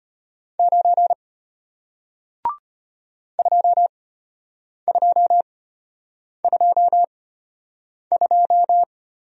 Morse code, heard as a keyed tone, about 33 wpm, spaced as for 8 wpm, 700 Hz. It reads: 9 2222